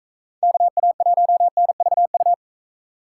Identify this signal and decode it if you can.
Morse code, heard as a keyed tone, about 39 wpm, 700 Hz. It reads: KA1NVU